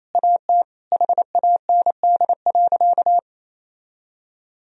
Morse code, heard as a keyed tone, about 28 wpm, 700 Hz. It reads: AT HAND.
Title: Sentences from Top 1000 Words - Review Part 2 - 28wpm